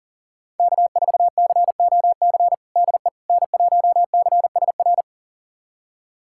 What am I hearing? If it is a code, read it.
Morse code, heard as a keyed tone, about 40 wpm, 700 Hz. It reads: K4COC DE N1CSR
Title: USA Callsign DE Callsign 40wpm